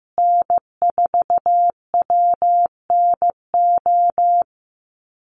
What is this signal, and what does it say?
Morse code, heard as a keyed tone, about 15 wpm, 700 Hz. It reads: N4WNO
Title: USA Call Signs Repeated 2x 15wpm